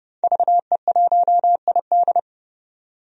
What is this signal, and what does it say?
Morse code, heard as a keyed tone, about 30 wpm, 700 Hz. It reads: VE1ID
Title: DXCC-100 Call Signs - 5 Characters - 30wpm